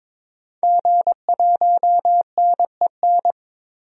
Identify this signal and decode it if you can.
Morse code, heard as a keyed tone, about 22 wpm, 700 Hz. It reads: G1NEN
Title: DXCC-100 Call Signs - 5 Characters - 22wpm